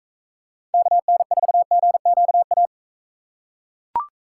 Morse code, heard as a keyed tone, about 45 wpm, 700 Hz. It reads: KN4GQA